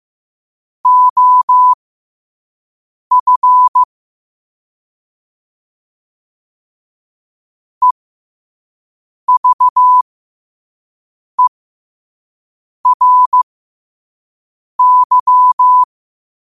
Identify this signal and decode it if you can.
Morse code, heard as a keyed tone, about 15 wpm, spaced as for 5 wpm, 1000 Hz. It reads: OF EVERY
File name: qotdfiles / 5wpm